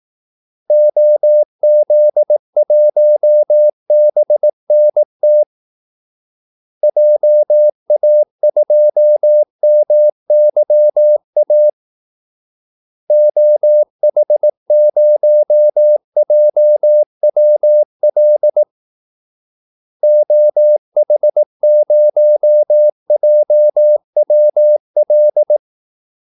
Morse code, heard as a keyed tone, about 18 wpm, 600 Hz. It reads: OZ1BNT JA2MYA OH0JWL OH0JWL
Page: Kaldesignaler 18 wpm